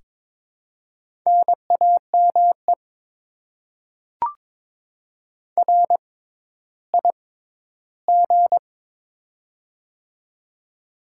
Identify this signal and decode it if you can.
Morse code, heard as a keyed tone, about 22 wpm, spaced as for 13 wpm, 700 Hz. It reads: NAME RIG
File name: Top 20 QSO Elements - 6x Character Spacing - 22wpm